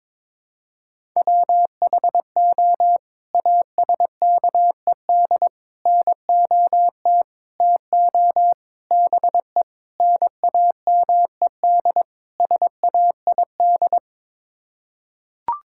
Morse code, heard as a keyed tone, about 22 wpm, 700 Hz. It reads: WHO ASKED NOT TO BE NAMED SAID
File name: practice / 22wpm / Sets of 7 Words - 22wpm